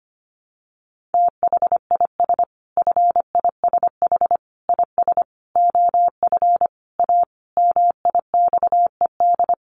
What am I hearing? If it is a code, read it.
Morse code, heard as a keyed tone, about 25 wpm, 700 Hz. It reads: THIS FISH IS OF A MIXED
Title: Sentences from Top 1500 Words 25wpm